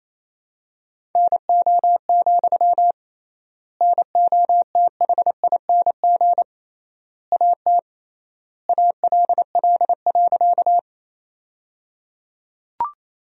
Morse code, heard as a keyed tone, about 28 wpm, 700 Hz. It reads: NO, NOTHING AT ALL.
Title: 3x Word-Spacing - Sentences from Top 500 Words - 28wpm